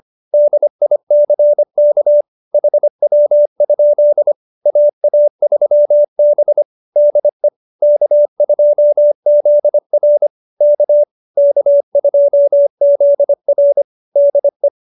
Morse code, heard as a keyed tone, about 25 wpm, 570 Hz. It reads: DICK HW? AA3B DE K2ZR K K2ZR DE
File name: QSO302 25